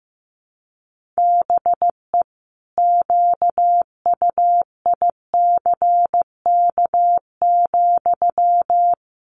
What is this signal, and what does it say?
Morse code, heard as a keyed tone, about 15 wpm, 700 Hz. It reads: BE QUICK,
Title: Sentences from Top 500 Words 15wpm